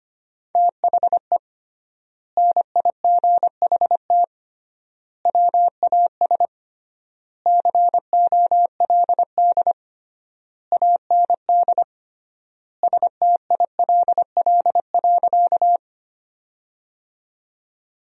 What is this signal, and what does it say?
Morse code, heard as a keyed tone, about 25 wpm, 700 Hz. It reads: THE NIGHT WAS COLD AND STILL.
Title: 3x Word-Spacing - Sentences from Top 400 Words - 25wpm